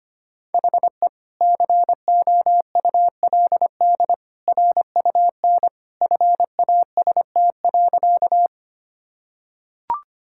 Morse code, heard as a keyed tone, about 25 wpm, 700 Hz. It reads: HE COULD RUN FAST.
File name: practice / 25wpm / Sentences with 2-Word Phrases - 25wpm